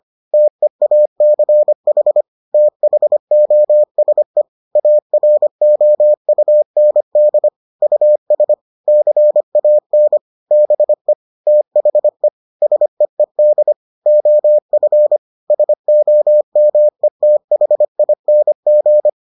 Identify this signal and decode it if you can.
Morse code, heard as a keyed tone, about 25 wpm, 600 Hz. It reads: TEACH THOSE AROUND US CAN BE THE SEED OF SOMETHING